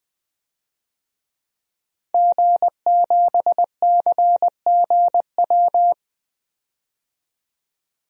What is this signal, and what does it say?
Morse code, heard as a keyed tone, about 20 wpm, 700 Hz. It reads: G7CGW
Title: DXCC-100 Call Signs - 5 Characters - 20wpm